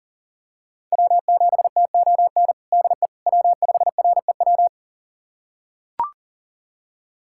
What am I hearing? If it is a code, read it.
Morse code, heard as a keyed tone, about 40 wpm, 700 Hz. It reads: W7TON DE W5REW
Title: USA Callsign DE Callsign - 5 Characters - 40wpm